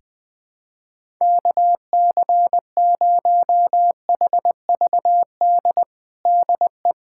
Morse code, heard as a keyed tone, about 21 wpm, 700 Hz. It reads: KC0HVD DE
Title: USA Callsign DE Callsign 20wpm